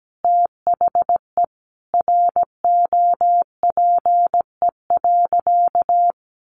Morse code, heard as a keyed tone, about 17 wpm, 700 Hz. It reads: THE ROPE.
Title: Easy Phrases 17wpm